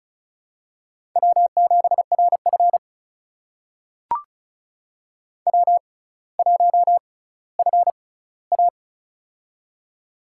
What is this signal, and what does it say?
Morse code, heard as a keyed tone, about 36 wpm, spaced as for 18 wpm, 700 Hz. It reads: W7RF W1FA
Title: USA Call Signs - 4 Characters - 6x Character Spacing - 35wpm